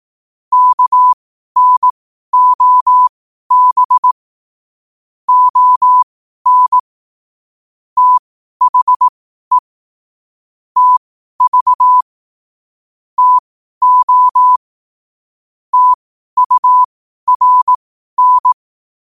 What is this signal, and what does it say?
Morse code, heard as a keyed tone, about 18 wpm, spaced as for 12 wpm, 1000 Hz. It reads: KNOB ON THE TV TO TURN
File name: qotdfiles / 12wpm